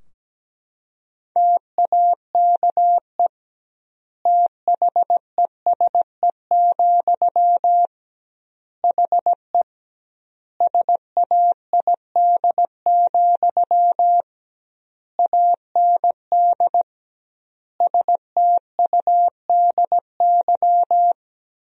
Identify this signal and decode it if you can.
Morse code, heard as a keyed tone, about 17 wpm, 700 Hz. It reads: TAKE THESE, HE SAID, AND STUDY
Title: 2x Word-Spacing - Sentences from Top 300 Words - 17wpm